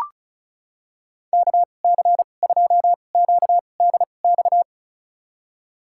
Morse code, heard as a keyed tone, about 34 wpm, spaced as for 25 wpm, 700 Hz. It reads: KC2QDX
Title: USA Call Signs - 6 Characters - 2x Character Spacing - 35wpm